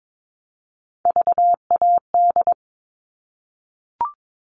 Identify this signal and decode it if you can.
Morse code, heard as a keyed tone, about 22 wpm, 700 Hz. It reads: VAD
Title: All Cut Numbers - 3-Digit 22wpm